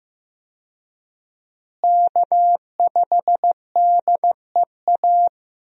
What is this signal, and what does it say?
Morse code, heard as a keyed tone, about 15 wpm, 700 Hz. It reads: K5DEA
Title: USA Call Signs - 5 Characters - Repeated 2x - 15wpm